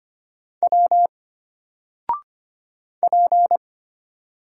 Morse code, heard as a keyed tone, about 25 wpm, 700 Hz. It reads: W P
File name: Single-Letter-Number-25wpm